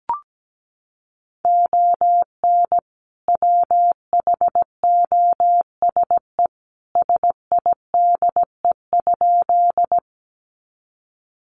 Morse code, heard as a keyed tone, about 17 wpm, 700 Hz. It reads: ON WHOSE SIDE?
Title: Sentences from Top 1200 Words 17wpm